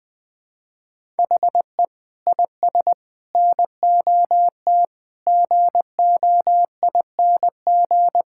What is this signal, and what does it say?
Morse code, heard as a keyed tone, about 20 wpm, 700 Hz. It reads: HE IS NOT GOING
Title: Sentences with 6-Word Phrases - Spoken Beginning - 20wpm